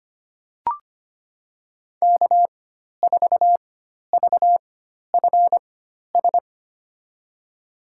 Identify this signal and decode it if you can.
Morse code, heard as a keyed tone, about 25 wpm, spaced as for 12 wpm, 700 Hz. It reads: K4VFS
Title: USA Call Signs - 5 Characters - 4x Character Spacing - 25wpm